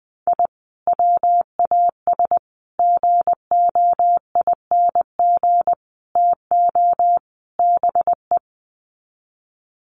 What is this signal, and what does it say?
Morse code, heard as a keyed tone, about 20 wpm, 700 Hz. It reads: I WAS GOING TO BE
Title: Sets of 5 Words 20wpm